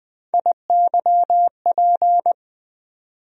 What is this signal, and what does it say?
Morse code, heard as a keyed tone, about 20 wpm, 700 Hz. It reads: IYP
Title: USA Call Sign Postfixes - 3 Letters 20wpm